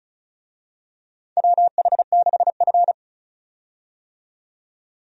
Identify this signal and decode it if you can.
Morse code, heard as a keyed tone, about 34 wpm, 700 Hz. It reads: WH6F
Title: USA Call Signs - 4 Characters - 35wpm